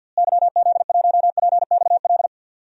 Morse code, heard as a keyed tone, about 52 wpm, 700 Hz. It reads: YZ1PXL
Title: DXCC-100 Call Signs - 6 Characters - Repeated 3x - 50wpm